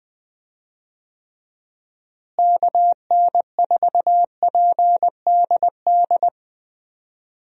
Morse code, heard as a keyed tone, about 20 wpm, 700 Hz. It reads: KN4PDD
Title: USA Call Signs - 6 Characters - Repeated 2x - 20wpm